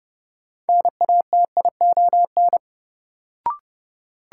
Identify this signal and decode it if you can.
Morse code, heard as a keyed tone, about 30 wpm, 700 Hz. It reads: NATION
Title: Sets of 3 Words - New Word Drill - 30wpm